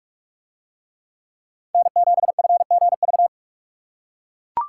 Morse code, heard as a keyed tone, about 44 wpm, 700 Hz. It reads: N7FGV